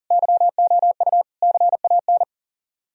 Morse code, heard as a keyed tone, about 37 wpm, 700 Hz. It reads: YOU CAN
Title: Speed-Racing - Sets of 5 Words 30wpm